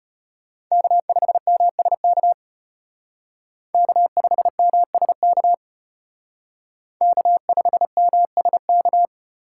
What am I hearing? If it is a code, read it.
Morse code, heard as a keyed tone, about 31 wpm, 700 Hz. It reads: K5MSK K5MSK K5MSK
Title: Speed Racing - USA Call Signs 25wpm